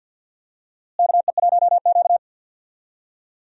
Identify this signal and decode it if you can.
Morse code, heard as a keyed tone, about 51 wpm, 700 Hz. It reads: KE1Q
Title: USA Call Signs - 4 Characters - Repeated 2x - 50wpm